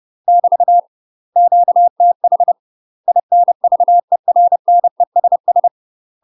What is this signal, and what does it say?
Morse code, heard as a keyed tone, about 30 wpm, 700 Hz. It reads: <BT> QTH INVERNESS
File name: UK MASTER QSOs TRAINING / 30WPM / UK MASTER QSOs 0045